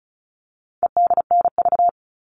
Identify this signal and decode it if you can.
Morse code, heard as a keyed tone, about 34 wpm, 700 Hz. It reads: EDNV
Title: All Cut Numbers - 4-Digit 35wpm